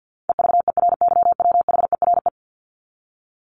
Morse code, heard as a keyed tone, about 48 wpm, 700 Hz. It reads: EVERYWHERE
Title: Top 1501-1750 Words 50wpm